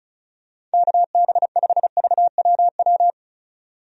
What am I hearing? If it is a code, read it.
Morse code, heard as a keyed tone, about 36 wpm, 700 Hz. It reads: KB5VWW
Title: USA Call Signs - 6 Characters - 35wpm